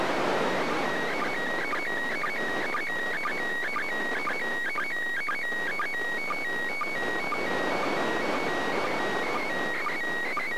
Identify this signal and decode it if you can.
SSTV (slow-scan television), PD 120.
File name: 2020.10.02 ISS